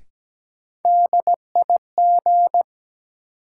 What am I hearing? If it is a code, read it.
Morse code, heard as a keyed tone, about 17 wpm, 700 Hz. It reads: DIG
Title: Sentences from Top 1600 Words - New Word Drill - 17wpm